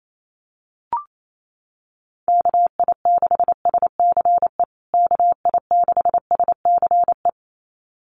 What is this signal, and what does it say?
Morse code, heard as a keyed tone, about 28 wpm, 700 Hz. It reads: KI6SCE KI6SCE
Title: USA Call Signs Repeated 2x 28wpm